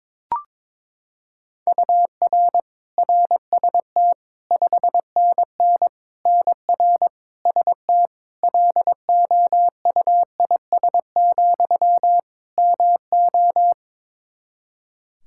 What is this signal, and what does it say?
Morse code, heard as a keyed tone, about 22 wpm, 700 Hz. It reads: UR RST 5NN NR ST LOUIS, MO
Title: Practice QSO - Exchange 1 - Beginning - 22wpm